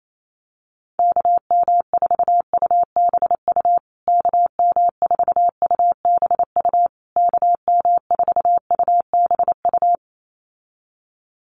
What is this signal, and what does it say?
Morse code, heard as a keyed tone, about 28 wpm, 700 Hz. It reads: KM4UBU KM4UBU KM4UBU
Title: USA Call Signs Repeated 3x 28wpm